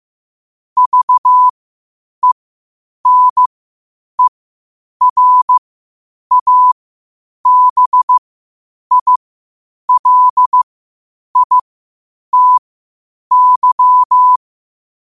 Morse code, heard as a keyed tone, about 15 wpm, spaced as for 8 wpm, 1000 Hz. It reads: VENERABILITY
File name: qotdfiles / 8wpm